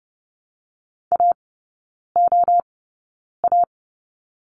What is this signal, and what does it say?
Morse code, heard as a keyed tone, about 30 wpm, spaced as for 9 wpm, 700 Hz. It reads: AOA
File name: 081 Lesson 14e - Number 2 - Three Characters 30wpm